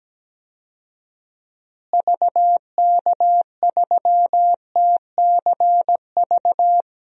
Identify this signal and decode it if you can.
Morse code, heard as a keyed tone, about 17 wpm, 700 Hz. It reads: VK3TCV
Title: DXCC-100 Call Signs - 6 Characters - Repeated 2x - 17wpm